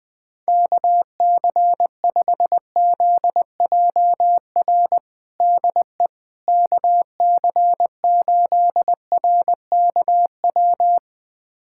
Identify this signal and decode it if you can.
Morse code, heard as a keyed tone, about 20 wpm, 700 Hz. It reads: KC5ZJR DE KC8RKW
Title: USA Callsign DE Callsign - 6 Characters - 20wpm